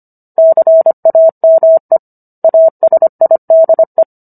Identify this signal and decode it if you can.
Morse code, heard as a keyed tone, about 25 wpm, 650 Hz. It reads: CAME ASIDE